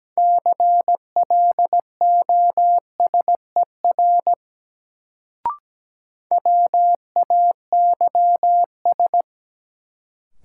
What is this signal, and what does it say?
Morse code, heard as a keyed tone, about 17 wpm, 700 Hz. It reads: CLOSER WAYS
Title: Sets of 5 Words Encore - New Word Drill - 17wpm